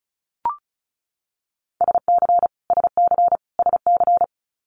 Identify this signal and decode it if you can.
Morse code, heard as a keyed tone, about 36 wpm, 700 Hz. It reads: SC SC SC
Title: Multi-letter Phonemes Repeated 3x 35wpm